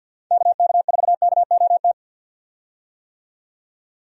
Morse code, heard as a keyed tone, about 52 wpm, 700 Hz. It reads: KK4KOT
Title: USA Call Signs - 6 Characters - 50wpm